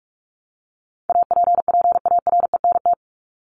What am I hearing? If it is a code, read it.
Morse code, heard as a keyed tone, about 44 wpm, 700 Hz. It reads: APPARENT
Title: LY Suffix - Top 100 Words with Base Words 45wpm